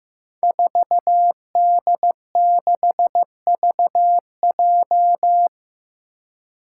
Morse code, heard as a keyed tone, about 15 wpm, 700 Hz. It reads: 4D6VJ
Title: Five Challenging Letter-Number Combinations 15wpm